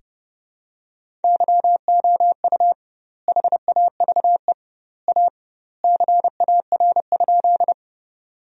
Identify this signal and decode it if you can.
Morse code, heard as a keyed tone, about 32 wpm, 700 Hz. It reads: YOU HAVE A CAR?
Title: 2x Word-Spacing - Sentences from Top 300 Words - 30wpm